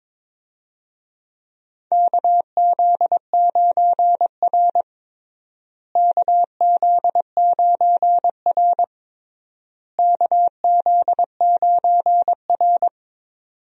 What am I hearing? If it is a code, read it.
Morse code, heard as a keyed tone, about 21 wpm, 700 Hz. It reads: KZ9R KZ9R KZ9R
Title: USA Call Signs - 4 Characters - Repeated 3x - 22wpm